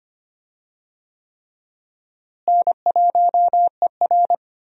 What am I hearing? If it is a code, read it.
Morse code, heard as a keyed tone, about 25 wpm, 700 Hz. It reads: N1ER
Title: USA Call Signs - 4 Characters - Repeated 3x - 25wpm